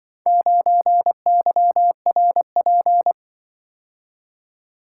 Morse code, heard as a keyed tone, about 24 wpm, 700 Hz. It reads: 9YRP
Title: Speed-Racing - Four Letter-Number Combinations - 20wpm